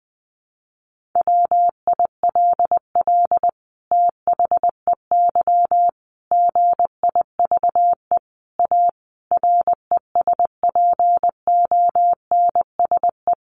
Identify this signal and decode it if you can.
Morse code, heard as a keyed tone, about 20 wpm, 700 Hz. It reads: WILL THEY GIVE A RESPONSE
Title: Sentences from Top 1400 Words 20wpm